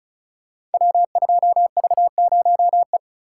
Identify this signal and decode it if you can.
Morse code, heard as a keyed tone, about 36 wpm, 700 Hz. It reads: W2V0E
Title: Code Groups - 5 Letter:Numbers - 35wpm